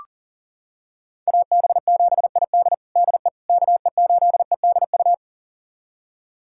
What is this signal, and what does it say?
Morse code, heard as a keyed tone, about 37 wpm, 700 Hz. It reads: AB7ID DE KE8EDU
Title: USA Callsign DE Callsign 40wpm